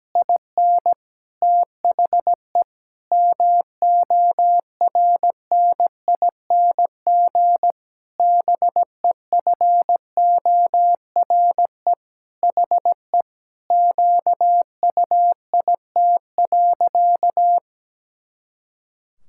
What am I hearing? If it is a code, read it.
Morse code, heard as a keyed tone, about 17 wpm, 700 Hz. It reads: IN THE MORNING BEFORE HE QUIT.
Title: Pangram Challenge v2 - 17wpm